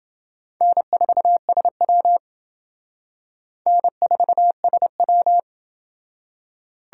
Morse code, heard as a keyed tone, about 28 wpm, 700 Hz. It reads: N4SW N4SW